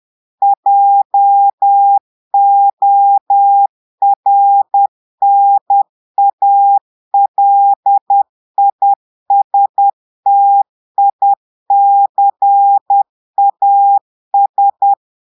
Morse code, heard as a keyed tone, about 10 wpm, 800 Hz. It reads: JORNALISTICAS
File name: portugais texte en clair 2 vitesse 10 mots minutes